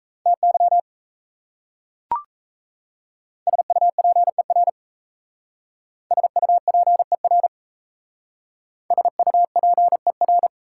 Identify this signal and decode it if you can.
Morse code, heard as a keyed tone, about 36 wpm, 700 Hz. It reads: TY SUPER SUPER SUPER